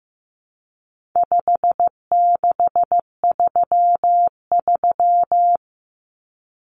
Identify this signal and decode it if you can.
Morse code, heard as a keyed tone, about 15 wpm, 700 Hz. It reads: 5633
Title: Numbers - 4-Digit 15wpm